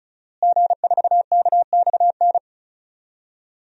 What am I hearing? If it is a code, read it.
Morse code, heard as a keyed tone, about 36 wpm, 700 Hz. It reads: G4KXN
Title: DXCC-100 Call Signs - 5 Characters - Repeated 2x - 35wpm